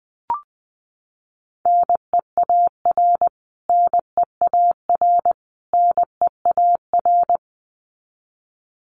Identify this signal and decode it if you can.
Morse code, heard as a keyed tone, about 20 wpm, 700 Hz. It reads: NEAR NEAR NEAR
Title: Top 500 Words Repeated 20wpm